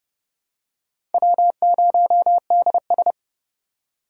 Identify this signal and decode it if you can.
Morse code, heard as a keyed tone, about 30 wpm, 700 Hz. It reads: W0DS
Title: USA Call Signs - 2x Character Spacing - 30wpm